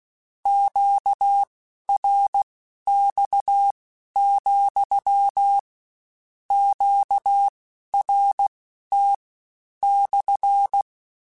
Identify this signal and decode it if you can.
Morse code, heard as a keyed tone, about 16 wpm, spaced as for 11 wpm, 800 Hz. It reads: QRX, QRT /
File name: Q Calls / Q Calls-mp3s / Q Calls SPSL2 @15